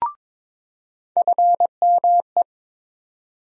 Morse code, heard as a keyed tone, about 22 wpm, 700 Hz. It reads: FME
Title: USA Call Sign Postfixes - 3 Letters 22wpm